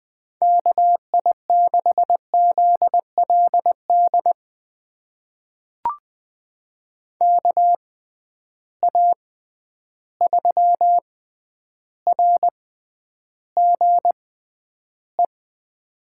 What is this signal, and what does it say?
Morse code, heard as a keyed tone, about 20 wpm, spaced as for 10 wpm, 700 Hz. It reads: KI6ZLD KA3RGE
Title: USA Call Signs - 6 Characters - 6x Character Spacing - 20wpm